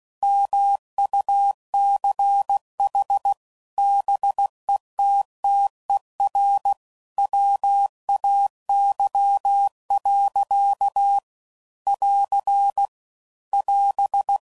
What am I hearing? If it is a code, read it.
Morse code, heard as a keyed tone, about 16 wpm, 800 Hz. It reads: MUCH BETTER WAY. <AR> <AS>
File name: CW Code Groups @15WPM / Technical Subjects / Gates Intro